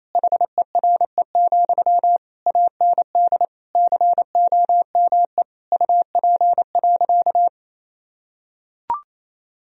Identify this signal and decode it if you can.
Morse code, heard as a keyed tone, about 28 wpm, 690 Hz. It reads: HERE, AND COME UP.
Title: Sentences from Top 1000 Words - Review Part 2 - 28wpm